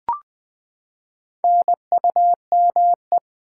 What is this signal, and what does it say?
Morse code, heard as a keyed tone, about 20 wpm, 700 Hz. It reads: NUME
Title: Build A Word - 4 Letters At A Time - 8-Letter Words - 20wpm